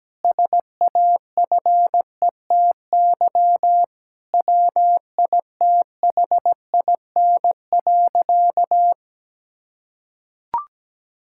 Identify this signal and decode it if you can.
Morse code, heard as a keyed tone, about 17 wpm, 700 Hz. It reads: SAFETY WITHIN.
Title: Sentences from Top 1600 Words 17wpm